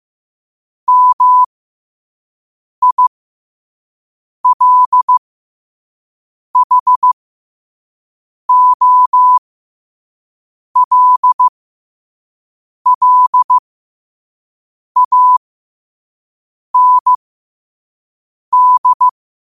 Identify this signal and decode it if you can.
Morse code, heard as a keyed tone, about 15 wpm, spaced as for 5 wpm, 1000 Hz. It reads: MILHOLLAND
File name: qotdfiles / 5wpm